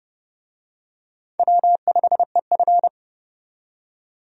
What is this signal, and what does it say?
Morse code, heard as a keyed tone, about 30 wpm, 700 Hz. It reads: W5EF
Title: USA Call Signs - 4 Characters - 30wpm